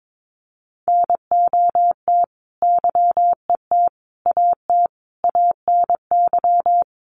Morse code, heard as a keyed tone, about 22 wpm, 700 Hz. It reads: NOT YET AT ANY
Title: Sentences from Top 700 Words 22wpm